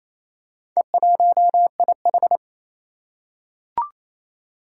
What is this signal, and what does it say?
Morse code, heard as a keyed tone, about 29 wpm, 700 Hz. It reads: E1IH
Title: Four Challenging Letter-Number Combinations 28wpm